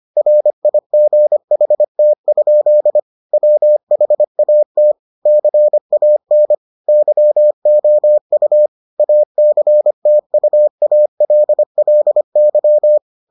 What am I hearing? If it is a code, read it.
Morse code, heard as a keyed tone, about 25 wpm, 600 Hz. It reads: RIGHT? WHAT CAN YOU ACTUALLY